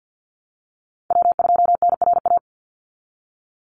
Morse code, heard as a keyed tone, about 50 wpm, 700 Hz. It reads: W2NRA
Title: Call Signs Repeated 2x - 50wpm